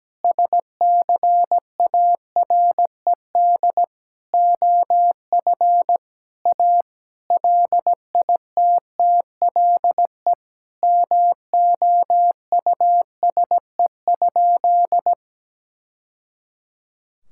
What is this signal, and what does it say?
Morse code, heard as a keyed tone, about 17 wpm, 700 Hz. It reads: SCARED OF A LITTLE MOUSE?
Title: Sentences from Top 1000 Words - Review Part 5 - 17wpm